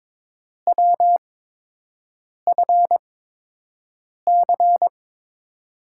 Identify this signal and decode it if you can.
Morse code, heard as a keyed tone, about 21 wpm, spaced as for 6 wpm, 700 Hz. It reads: WFC
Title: USA Call Signs - 5 Characters - 8x Character Spacing - 22wpm